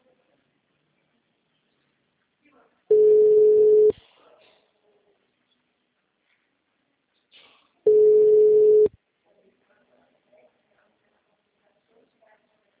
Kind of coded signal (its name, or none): telephone call progress tone